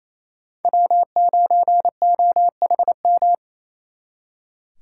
Morse code, heard as a keyed tone, about 36 wpm, 700 Hz. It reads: W9OHM